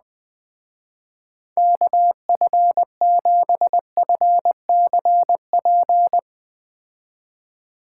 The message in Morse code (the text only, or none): KF7FCP